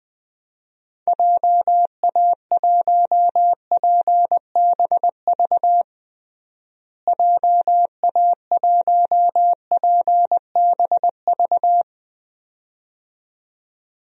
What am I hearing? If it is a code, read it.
Morse code, heard as a keyed tone, about 21 wpm, 700 Hz. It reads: JA1PBV JA1PBV
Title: DXCC-100 Call Signs - 6 Characters - Repeated 2x - 20wpm